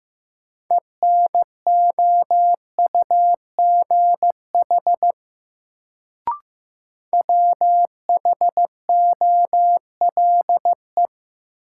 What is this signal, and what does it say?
Morse code, heard as a keyed tone, about 15 wpm, 700 Hz. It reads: ENOUGH WHOLE
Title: Sets of 2 Words - New Word Drill - 15wpm